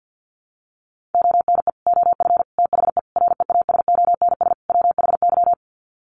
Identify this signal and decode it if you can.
Morse code, heard as a keyed tone, about 50 wpm, 700 Hz. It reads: ONE OF THE REASONS WHY
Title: Sets of 5 Words 50wpm